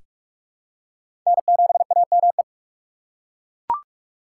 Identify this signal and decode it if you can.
Morse code, heard as a keyed tone, about 41 wpm, 700 Hz. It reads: N7AME
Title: K1USN SST QSO Practice - 45wpm